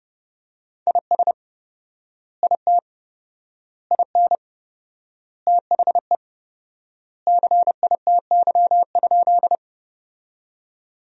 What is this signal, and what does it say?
Morse code, heard as a keyed tone, about 30 wpm, 700 Hz. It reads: IS IT IN THE CITY?
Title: 4x Word-Spacing - Sentences from Top 300 Words - 30wpm